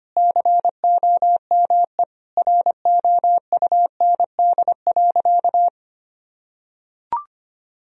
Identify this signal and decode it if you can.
Morse code, heard as a keyed tone, about 25 wpm, 700 Hz. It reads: COME ROUND.
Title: Sentences from Top 1000 Words - Review Part 1 - 25wpm